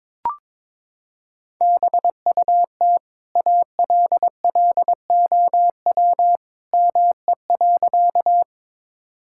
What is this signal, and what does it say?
Morse code, heard as a keyed tone, about 21 wpm, 700 Hz. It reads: BUT ALLOW ME.
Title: Sentences from Top 1000 Words - Review Part 5 - 22wpm